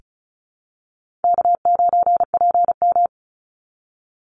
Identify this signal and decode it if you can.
Morse code, heard as a keyed tone, about 34 wpm, 700 Hz. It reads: K9PM
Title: NA SOTA Activator Callsigns 35wpm